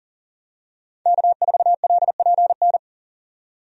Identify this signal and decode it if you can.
Morse code, heard as a keyed tone, about 37 wpm, 700 Hz. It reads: K4LPN
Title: USA Call Signs - 5 Characters - 40wpm